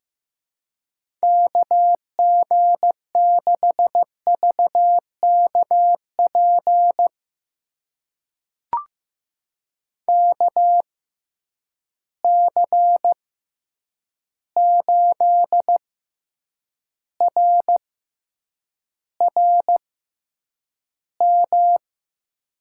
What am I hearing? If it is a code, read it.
Morse code, heard as a keyed tone, about 15 wpm, spaced as for 8 wpm, 700 Hz. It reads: KG6VKP KC8RRM
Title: USA Call Signs - 6 Characters - 6x Character Spacing - 15wpm